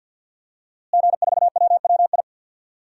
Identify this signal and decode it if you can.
Morse code, heard as a keyed tone, about 52 wpm, 700 Hz. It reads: G4WWI
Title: DXCC-100 Call Signs - 5 Characters - Repeated 3x - 50wpm